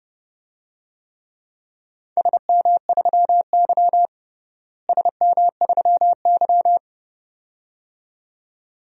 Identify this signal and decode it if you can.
Morse code, heard as a keyed tone, about 32 wpm, 700 Hz. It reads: SM3Y SM3Y
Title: DXCC-100 Call Signs - 4 Characters - Repeated 2x - 30wpm